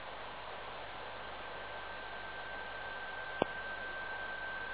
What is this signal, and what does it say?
PSK31, about 31.25 baud.